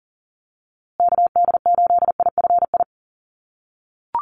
Morse code, heard as a keyed tone, about 40 wpm, 700 Hz. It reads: KD8IFI
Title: USA Call Signs Repeated 2x 40wpm